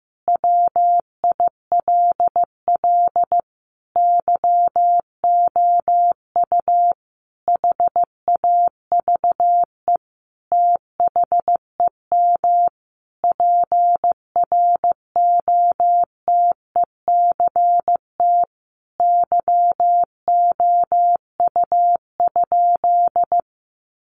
Morse code, heard as a keyed tone, about 15 wpm, 700 Hz. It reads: WILL YOU HAVE THEM PROTECT YOU?
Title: Sentences from Top 900 Words 15wpm